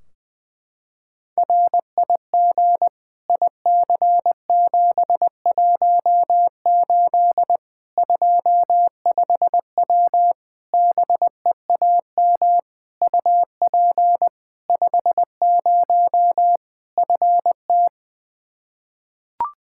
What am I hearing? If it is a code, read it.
Morse code, heard as a keyed tone, about 20 wpm, 700 Hz. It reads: RIG IC718 25W BEAM UP 50 FT
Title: K1USN SST QSO Practice 20wpm